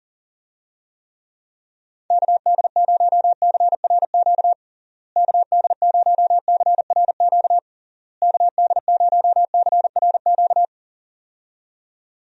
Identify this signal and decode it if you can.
Morse code, heard as a keyed tone, about 37 wpm, 700 Hz. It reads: KD0CRQ KD0CRQ KD0CRQ
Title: USA Call Signs - 6 Characters - Repeated 3x - 40wpm